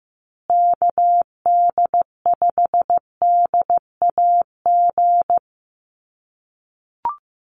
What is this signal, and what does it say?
Morse code, heard as a keyed tone, about 15 wpm, 700 Hz. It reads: KD5DAG
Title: USA Call Signs Repeated 2x 15wpm